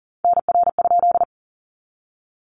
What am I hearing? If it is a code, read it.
Morse code, heard as a keyed tone, about 40 wpm, 700 Hz. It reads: NR?
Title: Common QSO Phrases 40wpm